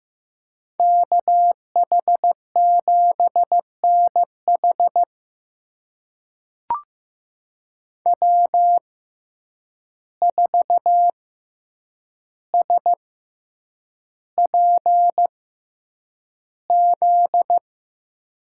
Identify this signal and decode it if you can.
Morse code, heard as a keyed tone, about 15 wpm, spaced as for 8 wpm, 700 Hz. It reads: KH7NH W4SPZ
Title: USA Call Signs - 6x Character Spacing - 15wpm